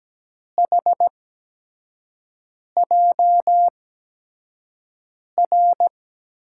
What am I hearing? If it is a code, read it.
Morse code, heard as a keyed tone, about 17 wpm, spaced as for 5 wpm, 700 Hz. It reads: HJR
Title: USA Call Signs - 6 Characters - 8x Character Spacing - 17wpm